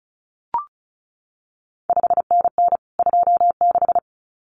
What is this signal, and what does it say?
Morse code, heard as a keyed tone, about 36 wpm, 700 Hz. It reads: 5NN 26
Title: CQ WW DX Contest Exchange Encore - 35wpm